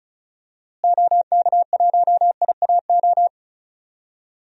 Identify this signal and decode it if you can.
Morse code, heard as a keyed tone, about 36 wpm, 700 Hz. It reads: OK1IAO
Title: DXCC-100 Call Signs - 6 Characters - Repeated 3x - 35wpm